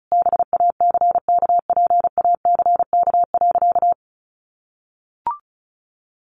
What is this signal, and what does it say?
Morse code, heard as a keyed tone, about 34 wpm, 700 Hz. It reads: BACKPACK.
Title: Additional Easy Phrases 35wpm